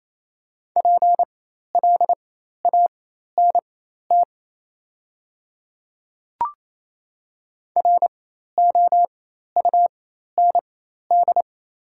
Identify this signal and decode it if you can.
Morse code, heard as a keyed tone, about 28 wpm, spaced as for 13 wpm, 700 Hz. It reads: PLANT ROUND
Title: 4x Character Spacing - Top 300 Words - 28wpm